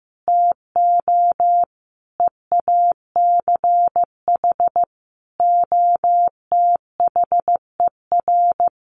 Morse code, heard as a keyed tone, about 15 wpm, 700 Hz. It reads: TO EACH OTHER
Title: Sentences from Top 1300 Words 15wpm